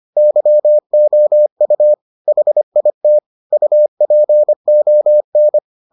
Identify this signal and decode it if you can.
Morse code, heard as a keyed tone, about 25 wpm, 600 Hz. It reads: YOU HIT UPON